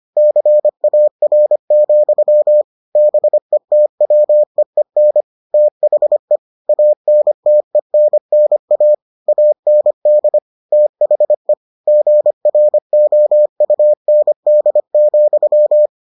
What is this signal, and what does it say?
Morse code, heard as a keyed tone, about 25 wpm, 600 Hz. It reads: CAR, BETWEEN THE ANTENNA AND THE GROUND,